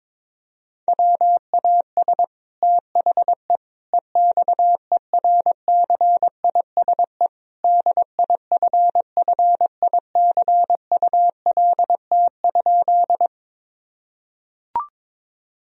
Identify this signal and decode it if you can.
Morse code, heard as a keyed tone, about 22 wpm, 700 Hz. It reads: WAS THE EXERCISE DIFFICULT?
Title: Sentences from Top 1000 Words - Review Part 4 - 22wpm